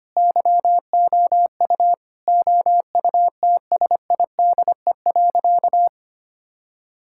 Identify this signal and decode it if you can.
Morse code, heard as a keyed tone, about 25 wpm, 700 Hz. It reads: YOU OUTSIDE.
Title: Sentences from Top 1000 Words - Review Part 6 - 25wpm